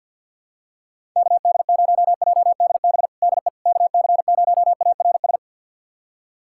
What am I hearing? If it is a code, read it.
Morse code, heard as a keyed tone, about 46 wpm, 700 Hz. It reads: KD0JDB DE KC0ARS
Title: USA Callsign DE Callsign - 6 Characters - 50wpm